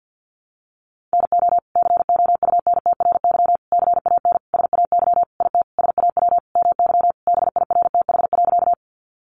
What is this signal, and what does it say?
Morse code, heard as a keyed tone, about 50 wpm, 700 Hz. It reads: NO COUNTRY CAN SAY IT SAW MY BIRTH.
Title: Sentences from Top 800 Words 50wpm